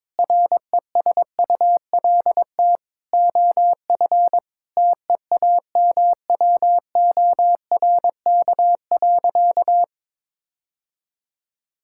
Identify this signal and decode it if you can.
Morse code, heard as a keyed tone, about 22 wpm, 700 Hz. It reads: RESULT OF TEAMWORK.
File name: Difficult Sentences with Hints - 22wpm